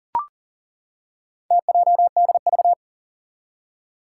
Morse code, heard as a keyed tone, about 40 wpm, 700 Hz. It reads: TJDV
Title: Four Challenging Letter-Number Combinations 40wpm